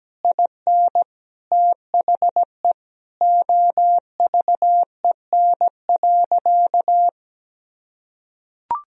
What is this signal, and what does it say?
Morse code, heard as a keyed tone, about 17 wpm, 700 Hz. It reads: IN THE OVEN.
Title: Sentences from Top 1000 Words - Review Part 4 - 17wpm